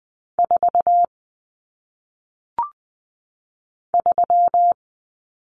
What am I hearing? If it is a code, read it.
Morse code, heard as a keyed tone, about 20 wpm, 700 Hz. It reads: 4 3
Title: Numbers - 1-Digit 20wpm